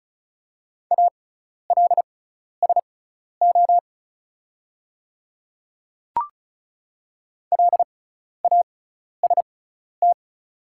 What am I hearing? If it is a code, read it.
Morse code, heard as a keyed tone, about 36 wpm, spaced as for 12 wpm, 700 Hz. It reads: ALSO LAST
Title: 6x Character Spacing - Top 300 Words - 35wpm